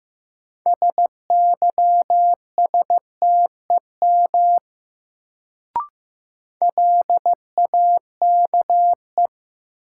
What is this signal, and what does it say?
Morse code, heard as a keyed tone, about 15 wpm, 700 Hz. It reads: SYSTEM LAKE
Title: Sentences from Top 1000 Words - Review Part 3 - New Word Drill - 15wpm